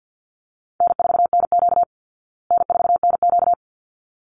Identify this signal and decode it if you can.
Morse code, heard as a keyed tone, about 50 wpm, 700 Hz. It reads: N4NQ N4NQ
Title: Easy Call Signs Repeated 2x - 50wpm